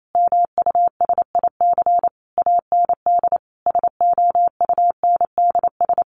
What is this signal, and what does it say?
Morse code, heard as a keyed tone, about 28 wpm, 700 Hz. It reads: MUSIC AND SOUNDS
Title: A Sound Of Thunder 28wpm